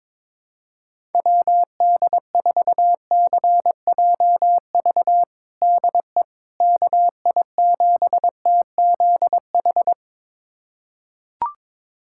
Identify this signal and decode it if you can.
Morse code, heard as a keyed tone, about 22 wpm, 700 Hz. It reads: WD4CJV DE KI7TZH
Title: USA Callsign DE Callsign - 6 Characters - 22wpm